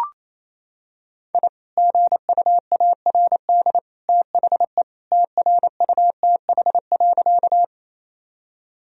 Morse code, heard as a keyed tone, about 28 wpm, 700 Hz. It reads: I GUARD THE TRUTH.
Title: Sentences from Top 1000 Words - Review Part 4 - 28wpm